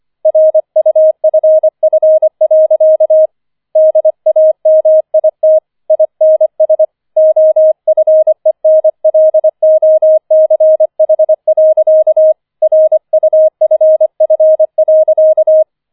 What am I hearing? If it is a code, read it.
Morse code, heard as a keyed tone, about 25 wpm, 630 Hz. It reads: RUFF. DAMIT INS OFENLOCH. RUFF.
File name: 25wpm